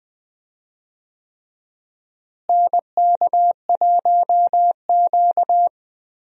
Morse code, heard as a keyed tone, about 20 wpm, 700 Hz. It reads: NK1Q
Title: USA Call Signs - 4 Characters - Repeated 3x - 20wpm